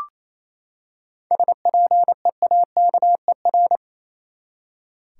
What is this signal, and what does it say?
Morse code, heard as a keyed tone, about 28 wpm, 700 Hz. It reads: SPEAKER
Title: Seven Letter Words - 28wpm